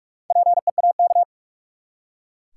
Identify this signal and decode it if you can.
Morse code, heard as a keyed tone, about 42 wpm, 700 Hz. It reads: PEAK
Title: Four Letter Words - 45wpm